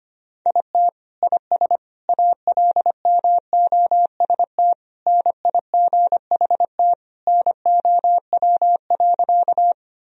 Morse code, heard as a keyed tone, about 25 wpm, 700 Hz. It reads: IT IS ALMOST NIGHT NOW.
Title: Sentences from Top 1000 Words - Review Part 4 - 25wpm